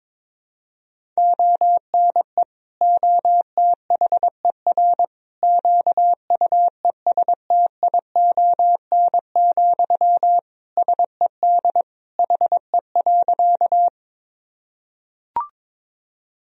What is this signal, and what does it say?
Morse code, heard as a keyed tone, about 22 wpm, 700 Hz. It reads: ONE OTHER QUESTION, SED HE.
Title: Sentences from Top 400 Words and Abbreviations 22wpm